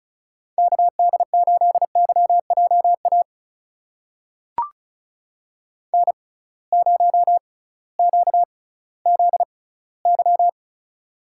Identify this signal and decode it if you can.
Morse code, heard as a keyed tone, about 36 wpm, spaced as for 19 wpm, 700 Hz. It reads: KD8YJA N0QZY
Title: USA Call Signs - 6x Character Spacing - 35wpm